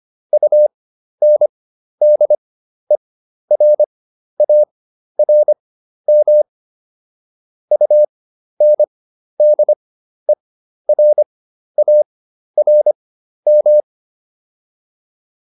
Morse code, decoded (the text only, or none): UNDERARM UNDERARM